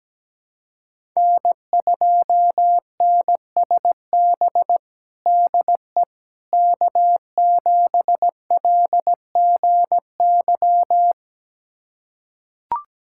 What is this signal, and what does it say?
Morse code, heard as a keyed tone, about 17 wpm, 700 Hz. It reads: N2NSB DE K7LGY
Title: USA Callsign DE Callsign - 5 Characters - 17wpm